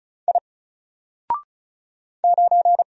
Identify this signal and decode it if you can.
Morse code, heard as a keyed tone, about 34 wpm, 700 Hz. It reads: I 9
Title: Single Letter-Number Alt 35wpm